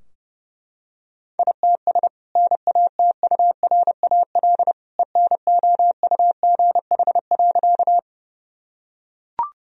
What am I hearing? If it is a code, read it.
Morse code, heard as a keyed tone, about 32 wpm, 700 Hz. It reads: ITS NATURAL ENOUGH.
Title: Sentences from Top 1000 Words - Review Part 4 - 30wpm